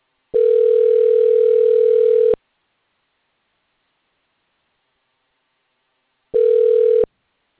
Ringback tone (North America), 440 + 480 Hz.